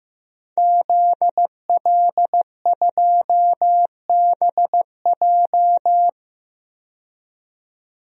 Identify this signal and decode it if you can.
Morse code, heard as a keyed tone, about 15 wpm, 700 Hz. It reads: ZL2BJ
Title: DXCC-100 Call Signs - 5 Characters - 15wpm